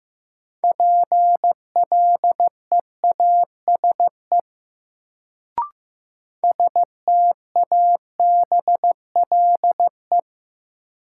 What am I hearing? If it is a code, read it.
Morse code, heard as a keyed tone, about 15 wpm, 700 Hz. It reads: PLEASE STABLE
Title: Sentences from Top 1500 Words - New Word Drill - 15wpm